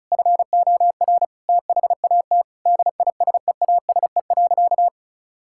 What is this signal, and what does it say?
Morse code, heard as a keyed tone, about 34 wpm, 700 Hz. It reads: FOR THAT DISEASE.
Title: Sentences from Top 1000 Words - Review Part 6 - 35wpm